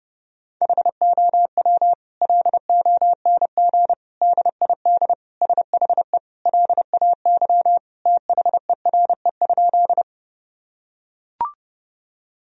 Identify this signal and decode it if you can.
Morse code, heard as a keyed tone, about 32 wpm, 700 Hz. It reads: HOW LONG DID SHE LAY THERE?
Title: Sentences from Top 1000 Words - Review Part 2 - 30wpm